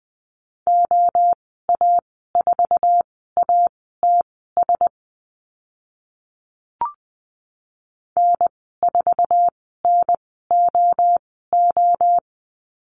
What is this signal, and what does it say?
Morse code, heard as a keyed tone, about 20 wpm, spaced as for 14 wpm, 700 Hz. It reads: OA4ATS N4NOO
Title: 069 Lesson 12g - Letter d - Call Signs 20wpm